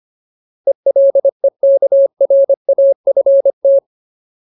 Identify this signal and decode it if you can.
Morse code, heard as a keyed tone, about 25 wpm, 550 Hz. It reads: ELEKRAFT
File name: Ultimate Word List / 25WPM Ultimate